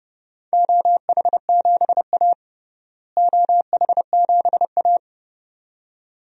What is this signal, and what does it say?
Morse code, heard as a keyed tone, about 32 wpm, 700 Hz. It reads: OH7A OH7A
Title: DXCC-100 Call Signs - 4 Characters - Repeated 2x - 30wpm